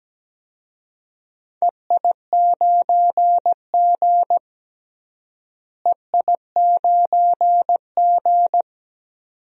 Morse code, heard as a keyed tone, about 17 wpm, 700 Hz. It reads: EI9G EI9G
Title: DXCC-100 Call Signs - 4 Characters - Repeated 2x - 17wpm